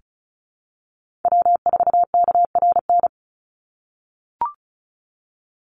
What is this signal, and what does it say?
Morse code, heard as a keyed tone, about 34 wpm, 700 Hz. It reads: W4KRN
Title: NA SOTA Chaser Callsigns 35wpm